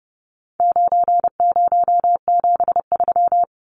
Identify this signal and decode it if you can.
Morse code, heard as a keyed tone, about 30 wpm, 700 Hz. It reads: 9073
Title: Numbers - 4-Digit 30wpm